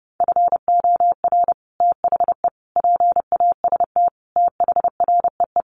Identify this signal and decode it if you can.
Morse code, heard as a keyed tone, about 30 wpm, 700 Hz. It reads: FOR THE PAST THREE
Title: Sets of 5 Words Encore2 30wpm